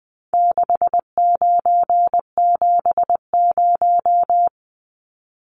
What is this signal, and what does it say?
Morse code, heard as a keyed tone, about 20 wpm, 700 Hz. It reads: 6970
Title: Numbers - 4-Digit 20wpm